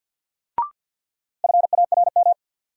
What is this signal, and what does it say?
Morse code, heard as a keyed tone, about 53 wpm, 700 Hz. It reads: FARM
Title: Sentences from Top 1200 Words - New Word Drill Alt - 50wpm